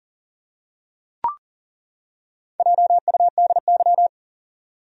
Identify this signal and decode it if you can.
Morse code, heard as a keyed tone, about 38 wpm, 700 Hz. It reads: JUDY
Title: USA Common Names - Max 4 Letters - 40wpm